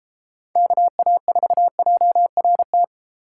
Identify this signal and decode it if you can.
Morse code, heard as a keyed tone, about 32 wpm, 700 Hz. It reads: KA4JRT